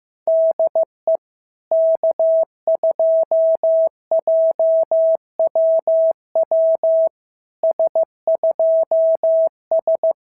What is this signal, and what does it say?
Morse code, heard as a keyed tone, about 15 wpm, 690 Hz. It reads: DE K2JWW S2S
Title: SOTA QSO Practice 15wpm